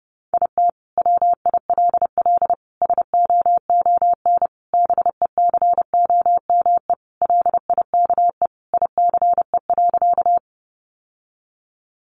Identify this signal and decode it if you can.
Morse code, heard as a keyed tone, about 28 wpm, 700 Hz. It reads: IT WILL SOON BECOME LIKE ICE.
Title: Sentences from Top 1000 Words 30wpm